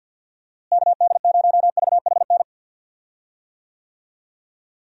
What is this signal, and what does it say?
Morse code, heard as a keyed tone, about 46 wpm, 700 Hz. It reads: KD0VHN
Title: USA Call Signs - 6 Characters - Repeated 2x - 50wpm